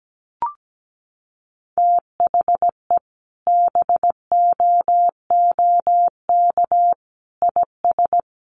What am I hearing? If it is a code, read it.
Morse code, heard as a keyed tone, about 17 wpm, 700 Hz. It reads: THE BOOK IS
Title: Sentences from Top 1300 Words 17wpm